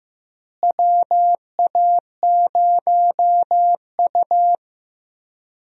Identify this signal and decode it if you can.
Morse code, heard as a keyed tone, about 15 wpm, 700 Hz. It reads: WA0U
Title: USA Call Signs - 4 Characters - Repeated 3x - 15wpm